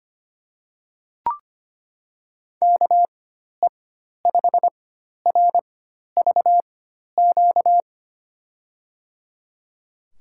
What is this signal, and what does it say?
Morse code, heard as a keyed tone, about 25 wpm, spaced as for 12 wpm, 700 Hz. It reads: KE5RVQ
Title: USA Call Signs - 6 Characters - 4x Character Spacing - 25wpm